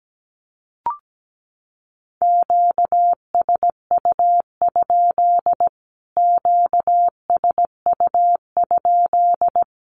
Morse code, heard as a keyed tone, about 17 wpm, 700 Hz. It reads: QSU? QSU?
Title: Q-Codes Repeated 3x 17wpm